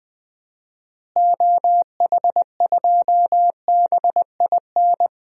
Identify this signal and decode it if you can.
Morse code, heard as a keyed tone, about 20 wpm, 700 Hz. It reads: OH2BIN
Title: DXCC-100 Call Signs - 6 Characters - Repeated 3x - 20wpm